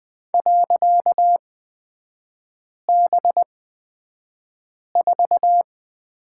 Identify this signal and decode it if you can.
Morse code, heard as a keyed tone, about 20 wpm, spaced as for 5 wpm, 700 Hz. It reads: .B4
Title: 255 Lesson 41e - Period - Three Characters 20wpm